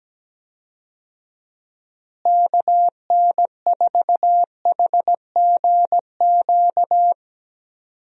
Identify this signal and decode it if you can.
Morse code, heard as a keyed tone, about 17 wpm, 700 Hz. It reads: KN4HGQ
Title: USA Call Signs - 6 Characters - Repeated 3x - 17wpm